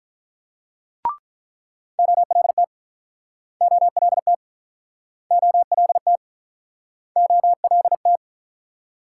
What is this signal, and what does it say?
Morse code, heard as a keyed tone, about 48 wpm, 700 Hz. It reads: OLT OLT OLT OLT